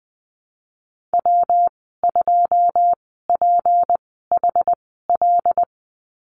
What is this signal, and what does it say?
Morse code, heard as a keyed tone, about 20 wpm, spaced as for 15 wpm, 700 Hz. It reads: W2PHL
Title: 167 Lesson 27g - Number 7 - Call Signs 20wpm